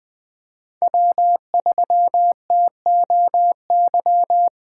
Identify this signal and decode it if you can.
Morse code, heard as a keyed tone, about 20 wpm, 700 Hz. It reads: W3TOY
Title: USA Call Signs - 5 Characters - 2x Character Spacing - 20wpm